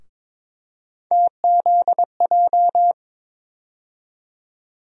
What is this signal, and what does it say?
Morse code, heard as a keyed tone, about 22 wpm, 700 Hz. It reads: TZJ
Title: USA Call Sign Postfixes - 3 Letters 22wpm